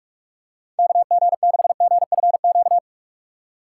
Morse code, heard as a keyed tone, about 46 wpm, 700 Hz. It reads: KG6GFQ
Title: USA Call Signs - 6 Characters - 4x Character Spacing - 45wpm